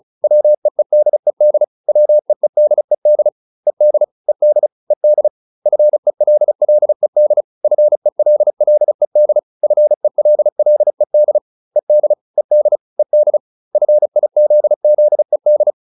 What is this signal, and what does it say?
Morse code, heard as a keyed tone, about 34 wpm, 600 Hz. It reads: WEEDED WEEDED ED ED ED FELLED FELLED FELLED ED ED ED FIZZED